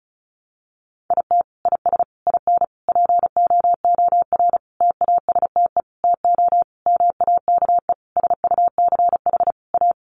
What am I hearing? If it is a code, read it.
Morse code, heard as a keyed tone, about 34 wpm, 700 Hz. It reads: IT IS IN POOR TASTE TO MAKE SUCH A
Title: Sentences from Top 1200 Words 35wpm